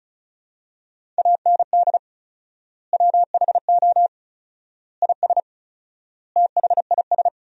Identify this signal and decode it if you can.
Morse code, heard as a keyed tone, about 36 wpm, 700 Hz. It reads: AND WHO IS THIS
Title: 4x Word-Spacing - Sentences from Top 300 Words - 35wpm